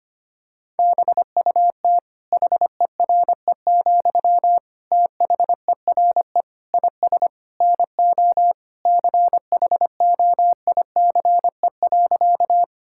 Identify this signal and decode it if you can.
Morse code, heard as a keyed tone, about 25 wpm, 700 Hz. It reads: BUT HERE, THERE IS NO CHOICE.
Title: Sentences from Top 1000 Words - Review Part 6 - 25wpm